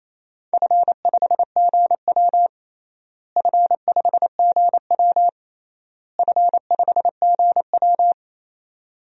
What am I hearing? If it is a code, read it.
Morse code, heard as a keyed tone, about 28 wpm, 700 Hz. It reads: F5GW F5GW F5GW
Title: DXCC-100 Call Signs - 4 Characters - Repeated 3x - 28wpm